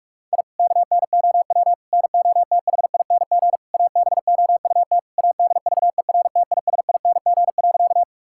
Morse code, heard as a keyed tone, about 41 wpm, 700 Hz. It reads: I KNOW NOTHING ABOUT ADVERTISING.
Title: Sentences from Top 1000 Words - Review Part 6 - 45wpm